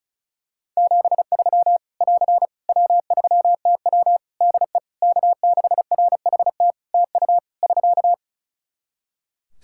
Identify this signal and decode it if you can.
Morse code, heard as a keyed tone, about 34 wpm, 700 Hz. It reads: 73 <AR> W3TW DE K6RHT TU <SK>
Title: Practice QSO - Exchange 4 - Ending - 35wpm